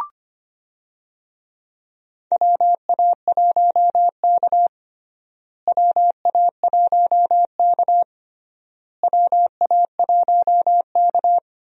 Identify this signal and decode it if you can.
Morse code, heard as a keyed tone, about 25 wpm, 700 Hz. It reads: WA1K WA1K WA1K
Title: USA Call Signs - 4 Characters - Repeated 3x - 25wpm